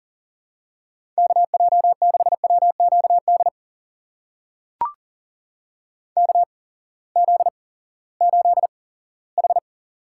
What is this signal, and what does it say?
Morse code, heard as a keyed tone, about 40 wpm, spaced as for 20 wpm, 700 Hz. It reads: KJ6WQD KZ8H
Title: USA Call Signs - 8x Character Spacing - 40wpm